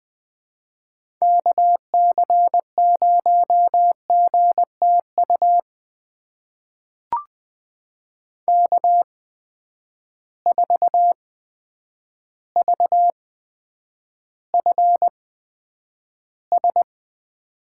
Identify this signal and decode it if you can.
Morse code, heard as a keyed tone, about 20 wpm, spaced as for 9 wpm, 700 Hz. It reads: KC0GTU K4VFS
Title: USA Call Signs - 8x Character Spacing - 20wpm